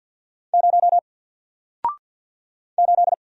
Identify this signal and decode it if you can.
Morse code, heard as a keyed tone, about 50 wpm, 700 Hz. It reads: 0 8
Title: Single Letter-Number 50wpm